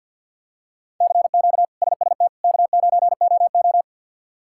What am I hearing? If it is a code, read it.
Morse code, heard as a keyed tone, about 50 wpm, 700 Hz. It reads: CQ SST K9OO
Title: K1USN SST QSO Practice - 50wpm